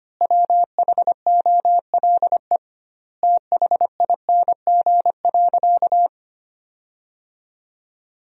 Morse code, heard as a keyed tone, about 25 wpm, 700 Hz. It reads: WHOLE THING.
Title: 2x Word-Spacing - Sentences from Top 400 Words - 25wpm